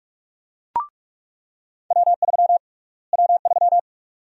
Morse code, heard as a keyed tone, about 41 wpm, 700 Hz. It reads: W3 W3
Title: USA Call Sign Prefixes - Repeated 2x - 45wpm